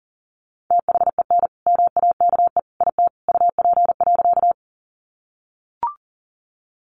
Morse code, heard as a keyed tone, about 40 wpm, 700 Hz. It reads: THEN MAKE IT UP.
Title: Sentences from Top 100 Words 40wpm